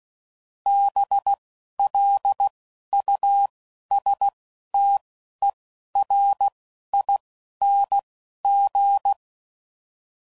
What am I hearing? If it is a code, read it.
Morse code, heard as a keyed tone, about 16 wpm, 800 Hz. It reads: B L U S T E R I N G